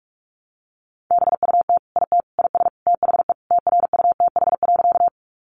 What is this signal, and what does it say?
Morse code, heard as a keyed tone, about 44 wpm, 700 Hz. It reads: BUT IT IS THE TRUTH.
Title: Sentences from Top 800 Words 45wpm